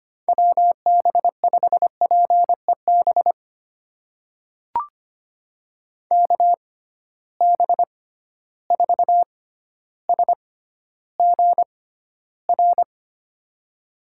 Morse code, heard as a keyed tone, about 25 wpm, spaced as for 13 wpm, 700 Hz. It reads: WB5PEB KB4SGR
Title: USA Call Signs - 6 Characters - 6x Character Spacing - 25wpm